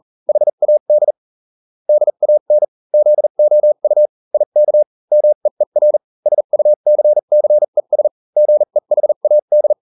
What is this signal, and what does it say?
Morse code, heard as a keyed tone, about 40 wpm, spaced as for 35 wpm, 600 Hz. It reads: HAD DAN ZOU IK MEER SUCCES GEHAD